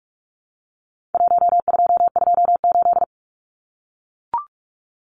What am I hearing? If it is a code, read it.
Morse code, heard as a keyed tone, about 44 wpm, 700 Hz. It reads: 1228
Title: Numbers - Contest Style - 4-Digit 45wpm